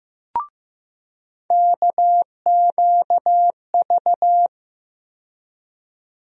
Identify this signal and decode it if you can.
Morse code, heard as a keyed tone, about 15 wpm, 700 Hz. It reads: KQV
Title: USA Call Sign Postfixes - 3 Letters 15wpm